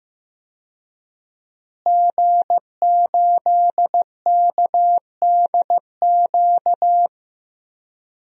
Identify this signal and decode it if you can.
Morse code, heard as a keyed tone, about 15 wpm, 700 Hz. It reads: G8KDQ
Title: DXCC-100 Call Signs - 5 Characters - Repeated 2x - 15wpm